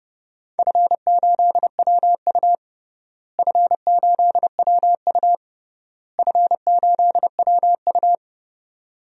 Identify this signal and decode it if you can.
Morse code, heard as a keyed tone, about 32 wpm, 700 Hz. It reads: F8WU F8WU F8WU
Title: DXCC-100 Call Signs - 4 Characters - Repeated 3x - 30wpm